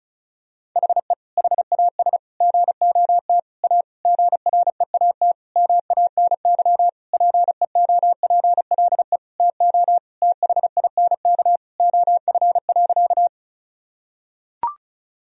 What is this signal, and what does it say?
Morse code, heard as a keyed tone, about 35 wpm, 700 Hz. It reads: HE HAS GOT A GREAT MANY PEOPLE TO THINK OF.
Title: Sentences from Top 1000 Words - Review Part 1 - 35wpm